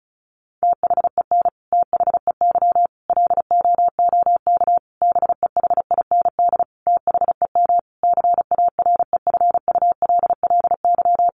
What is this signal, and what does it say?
Morse code, heard as a keyed tone, about 34 wpm, 700 Hz. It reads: THEN THEY LOOK BEHIND THEM CAREFULLY
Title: Sentences from Top 1300 Words 35wpm